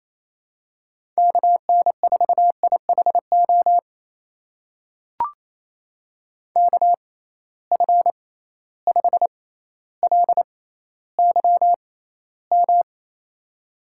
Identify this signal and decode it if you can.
Morse code, heard as a keyed tone, about 28 wpm, spaced as for 14 wpm, 700 Hz. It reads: KN4IHO KF5LYM
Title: USA Call Signs - 6 Characters - 6x Character Spacing - 28wpm